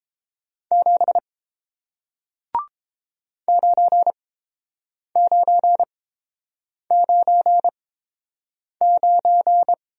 Morse code, heard as a keyed tone, about 30 wpm, spaced as for 7 wpm, 700 Hz. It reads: 7 9999